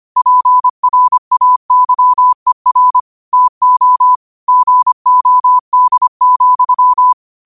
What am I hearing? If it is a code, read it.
Morse code, heard as a keyed tone, about 25 wpm, 1000 Hz. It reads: PRAYER TO GOD,